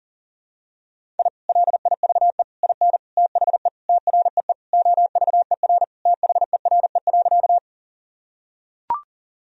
Morse code, heard as a keyed tone, about 40 wpm, 700 Hz. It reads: I LIVE IN THE TREE OVER THERE.
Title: Sentences from Top 1000 Words - Review Part 1 - 40wpm